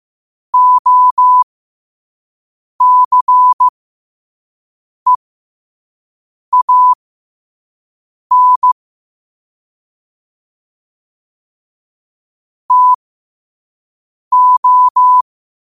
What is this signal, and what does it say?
Morse code, heard as a keyed tone, about 15 wpm, spaced as for 5 wpm, 1000 Hz. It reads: OCEAN TO
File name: qotdfiles / 5wpm